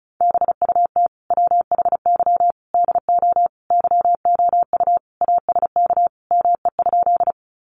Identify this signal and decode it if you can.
Morse code, heard as a keyed tone, about 34 wpm, 700 Hz. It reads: BUT WHY DO YOU ASK ME?
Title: Sentences from Top 200 Words 35wpm